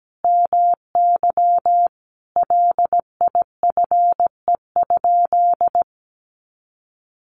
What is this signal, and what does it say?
Morse code, heard as a keyed tone, about 17 wpm, 700 Hz. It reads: MY LIFE?
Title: Sentences from Top 300 Words 17wpm